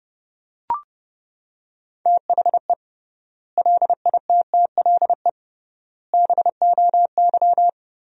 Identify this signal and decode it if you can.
Morse code, heard as a keyed tone, about 30 wpm, 700 Hz. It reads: THE LITTLE BOY
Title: 3x Word-Spacing - Sentences from Top 500 Words - 30wpm